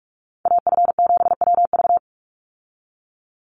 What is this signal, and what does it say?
Morse code, heard as a keyed tone, about 46 wpm, 690 Hz. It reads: AF7WV
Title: USA Call Signs 45wpm